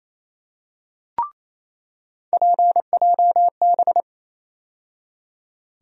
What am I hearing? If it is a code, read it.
Morse code, heard as a keyed tone, about 28 wpm, 700 Hz. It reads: PJB
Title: USA Call Sign Postfixes - 3 Letters 28wpm